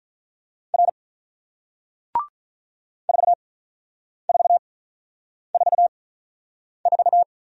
Morse code, heard as a keyed tone, about 48 wpm, spaced as for 9 wpm, 700 Hz. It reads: R 4444